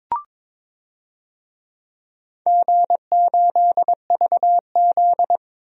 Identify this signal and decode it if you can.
Morse code, heard as a keyed tone, about 22 wpm, 700 Hz. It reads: G8VZ
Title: DXCC-100 Call Signs - 4 Characters - 22wpm